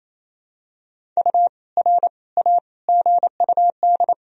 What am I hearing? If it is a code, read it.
Morse code, heard as a keyed tone, about 28 wpm, 700 Hz. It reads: U R A GUD
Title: Sentences from Top 200 Words and Abbreviations 28wpm